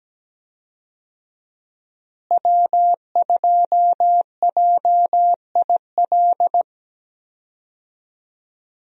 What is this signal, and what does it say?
Morse code, heard as a keyed tone, about 17 wpm, 700 Hz. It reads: W2JIL
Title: USA Call Signs - 5 Characters - Repeated 3x - 17wpm